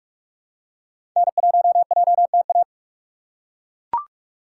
Morse code, heard as a keyed tone, about 42 wpm, 700 Hz. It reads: N1JTA
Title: USA Call Signs - 6x Character Spacing - 45wpm